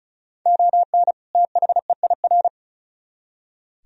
Morse code, heard as a keyed tone, about 43 wpm, 700 Hz. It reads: ON THEIR
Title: Speed-Racing - Sets of 2 Words 35wpm